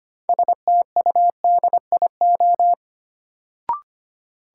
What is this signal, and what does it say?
Morse code, heard as a keyed tone, about 25 wpm, 700 Hz. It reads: STUDIO
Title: Sentences from Top 1200 Words - New Word Drill Alt - 25wpm